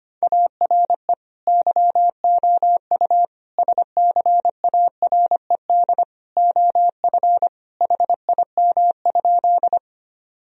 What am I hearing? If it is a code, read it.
Morse code, heard as a keyed tone, about 25 wpm, 690 Hz. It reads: ARE YOU SCARED OF HIM?
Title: Sentences from Top 1000 Words - Review Part 4 - 25wpm